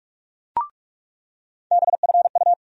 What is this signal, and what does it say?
Morse code, heard as a keyed tone, about 42 wpm, 700 Hz. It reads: BFU
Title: USA Call Sign Postfixes - 3 Letters 45wpm